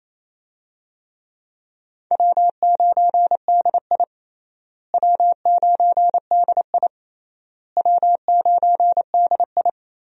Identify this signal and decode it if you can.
Morse code, heard as a keyed tone, about 28 wpm, 700 Hz. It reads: W9DI W9DI W9DI
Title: USA Call Signs - 4 Characters - Repeated 3x - 28wpm